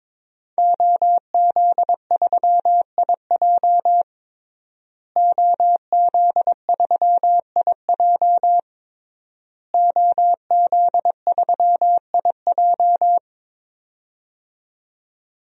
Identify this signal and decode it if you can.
Morse code, heard as a keyed tone, about 22 wpm, 700 Hz. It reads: OZ3IJ OZ3IJ OZ3IJ
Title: DXCC-100 Call Signs - 5 Characters - Repeated 3x - 22wpm